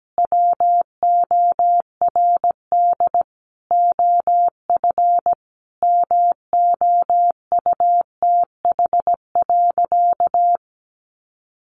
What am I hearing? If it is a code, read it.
Morse code, heard as a keyed tone, about 17 wpm, 700 Hz. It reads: WORD OF MOUTH.
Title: Sentences from Top 1500 Words 17wpm